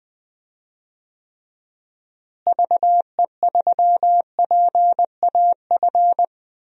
Morse code, heard as a keyed tone, about 21 wpm, 700 Hz. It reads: VE3PAF
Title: DXCC-100 Call Signs - 6 Characters - Repeated 3x - 20wpm